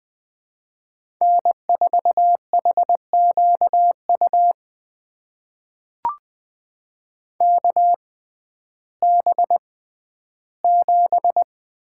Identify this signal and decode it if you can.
Morse code, heard as a keyed tone, about 20 wpm, spaced as for 12 wpm, 700 Hz. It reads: N4HQU KB7
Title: USA Call Signs - 6x Character Spacing - 20wpm